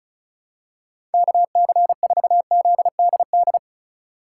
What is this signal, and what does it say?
Morse code, heard as a keyed tone, about 36 wpm, 700 Hz. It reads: KC4ZDD
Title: USA Call Signs - 2x Character Spacing - 35wpm